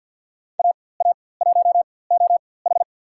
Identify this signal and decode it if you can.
Morse code, heard as a keyed tone, about 47 wpm, spaced as for 23 wpm, 700 Hz. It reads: AA1OH
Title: USA Call Signs - 4x Character Spacing - 50wpm